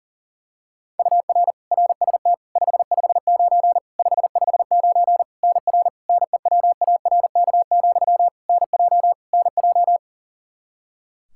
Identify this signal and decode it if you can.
Morse code, heard as a keyed tone, about 40 wpm, 700 Hz. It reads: UR RST 559 559 NR NEWARK, NJ NJ
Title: K1USN SST QSO Practice 40wpm